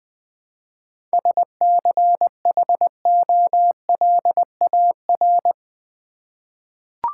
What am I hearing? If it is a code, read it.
Morse code, heard as a keyed tone, about 20 wpm, 700 Hz. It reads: SCHOLAR
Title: 7 Letter Words - First 3 Letters Given - 20wpm